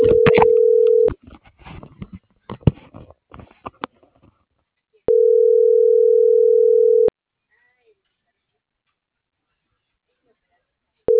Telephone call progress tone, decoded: ringback tone

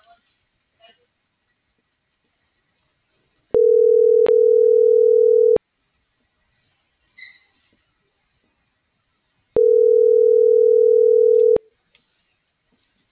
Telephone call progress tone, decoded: ringback tone

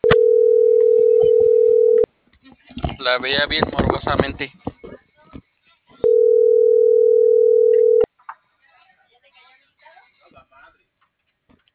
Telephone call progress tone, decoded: ringback tone

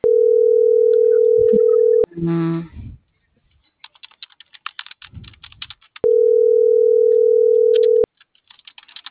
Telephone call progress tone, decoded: ringback tone

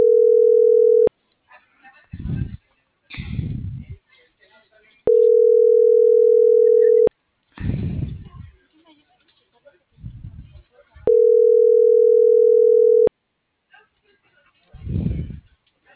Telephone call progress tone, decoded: ringback tone

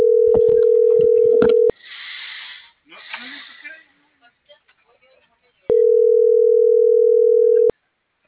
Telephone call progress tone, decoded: ringback tone

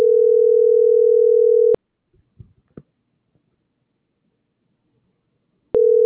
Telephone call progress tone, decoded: ringback tone

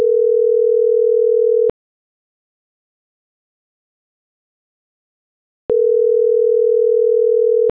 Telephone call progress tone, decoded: ringback tone

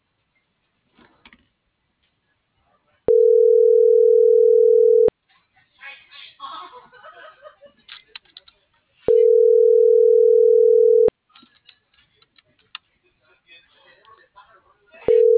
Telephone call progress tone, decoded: ringback tone

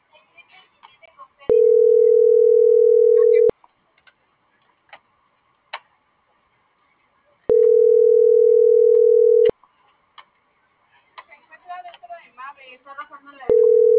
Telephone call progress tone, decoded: ringback tone

